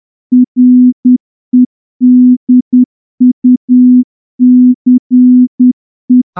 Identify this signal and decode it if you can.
Morse code, heard as a keyed tone, about 10 wpm, 250 Hz. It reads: REDUCE
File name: morse-training-10WPM